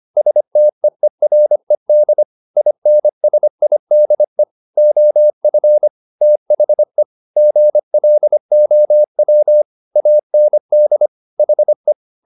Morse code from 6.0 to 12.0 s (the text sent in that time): THE GLOW AND HE